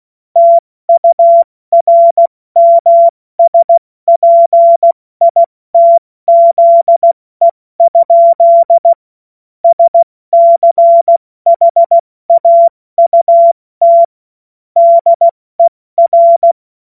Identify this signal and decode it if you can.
Morse code, heard as a keyed tone, about 16 wpm, 680 Hz. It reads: TURMSPITZE? SCHAUT DER
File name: Morsekurs / Witze / Witz 20 Februar 2024 70 F80